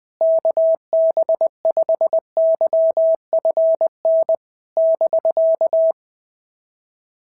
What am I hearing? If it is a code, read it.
Morse code, heard as a keyed tone, about 20 wpm, 650 Hz. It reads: KB5YFN <BK>
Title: SOTA QSO Practice 20wpm